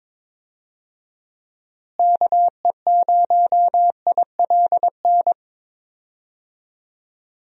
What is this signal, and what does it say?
Morse code, heard as a keyed tone, about 22 wpm, 700 Hz. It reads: KE0ILN
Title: USA Call Signs - 6 Characters - Repeated 2x - 22wpm